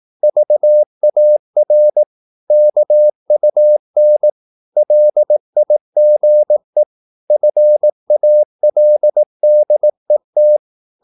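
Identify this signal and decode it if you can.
Morse code, heard as a keyed tone, about 18 wpm, 600 Hz. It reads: VAR KUN LIGE FALDET
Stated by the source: Slow (QRS) Morse practice at 18 wpm, CW med Gnister.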